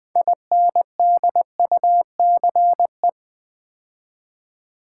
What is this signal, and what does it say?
Morse code, heard as a keyed tone, about 20 wpm, 700 Hz. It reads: INDUCE
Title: 6 Letter Words - First 3 Letters Given - 20wpm